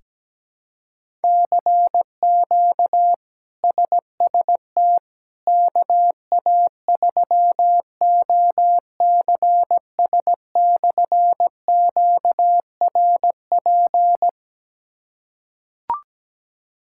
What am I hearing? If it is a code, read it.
Morse code, heard as a keyed tone, about 17 wpm, 700 Hz. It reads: CQ SST KA3OCS/QRP
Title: K1USN SST QSO Practice - 17wpm